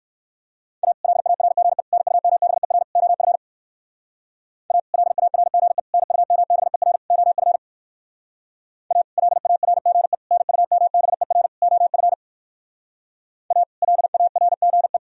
Morse code, heard as a keyed tone, about 58 wpm, 700 Hz. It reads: A LARGE NUMBER OF A LARGE NUMBER OF A LARGE NUMBER OF A LARGE